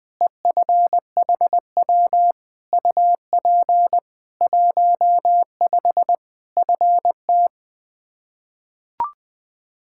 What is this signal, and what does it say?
Morse code, heard as a keyed tone, about 20 wpm, 700 Hz. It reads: EFHW UP 15 FT
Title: Practice QSO - Exchange 2 - Info - 20wpm